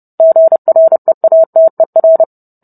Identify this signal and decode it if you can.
Morse code, heard as a keyed tone, about 30 wpm, 650 Hz. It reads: GREATER